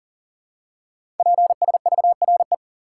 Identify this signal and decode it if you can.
Morse code, heard as a keyed tone, about 41 wpm, 690 Hz. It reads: PSVRE
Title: Code Groups - 5 Letter:Numbers - 40wpm